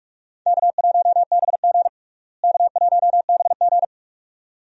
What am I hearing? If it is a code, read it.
Morse code, heard as a keyed tone, about 46 wpm, 700 Hz. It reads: K1BG K1BG
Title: USA Call Signs - 4 Characters - Repeated 2x - 45wpm